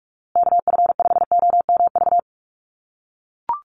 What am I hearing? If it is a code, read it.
Morse code, heard as a keyed tone, about 44 wpm, 700 Hz. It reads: KF5OMV
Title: USA Call Signs Repeated 2x 45wpm